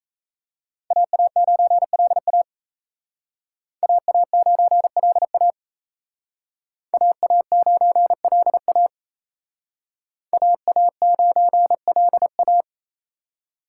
Morse code, heard as a keyed tone, about 37 wpm, 700 Hz. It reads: AA9LA AA9LA AA9LA AA9LA